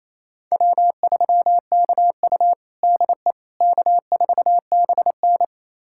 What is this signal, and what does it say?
Morse code, heard as a keyed tone, about 28 wpm, 700 Hz. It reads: W3KU DE K4BN
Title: USA Callsign DE Callsign - 4 Characters - 28wpm